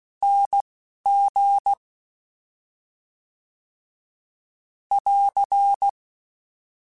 Morse code, heard as a keyed tone, about 16 wpm, spaced as for 11 wpm, 800 Hz. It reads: NG <AR>
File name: Special INGs-mp3s / Special INGs-mp3s / Slow ing @10